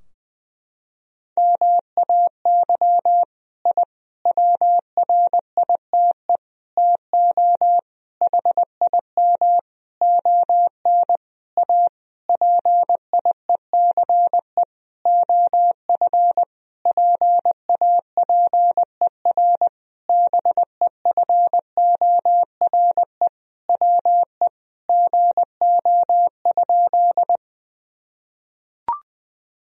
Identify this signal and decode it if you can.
Morse code, heard as a keyed tone, about 20 wpm, 700 Hz. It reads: MAY I WRITE TO HIM ON A PIECE OF PAPER BEFORE WE GO?
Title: Sentences from Top 1000 Words - Review Part 2 - 20wpm